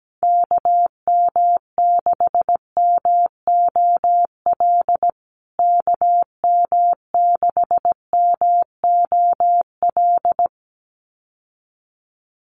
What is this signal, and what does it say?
Morse code, heard as a keyed tone, about 17 wpm, 700 Hz. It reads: KM6MOL KM6MOL
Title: USA Call Signs Repeated 2x 17wpm